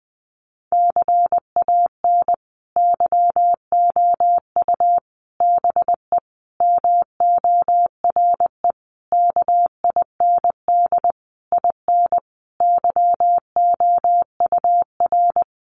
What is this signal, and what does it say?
Morse code, heard as a keyed tone, about 20 wpm, 700 Hz. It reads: CAN YOU BE MORE KIND IN YOUR
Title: Sentences from Top 1100 Words 20wpm